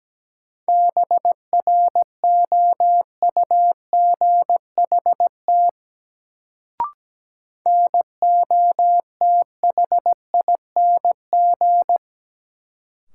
Morse code, heard as a keyed tone, about 17 wpm, 700 Hz. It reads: BROUGHT NOTHING
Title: Sentences from Top 500 Words - New Word Drill - 17wpm